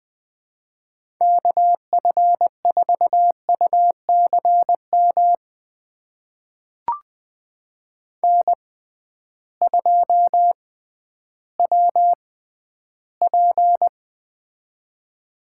Morse code, heard as a keyed tone, about 20 wpm, spaced as for 12 wpm, 700 Hz. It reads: KF4UCM N2WP
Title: USA Call Signs - 6x Character Spacing - 20wpm